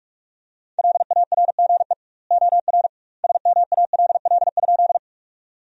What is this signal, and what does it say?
Morse code, heard as a keyed tone, about 44 wpm, 700 Hz. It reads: LARGE OR SMALL?
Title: 2x Word-Spacing - Sentences from Top 200 Words - 45wpm